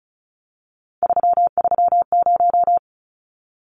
Morse code, heard as a keyed tone, about 34 wpm, 700 Hz. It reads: 330
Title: Numbers - 3-Digit 35wpm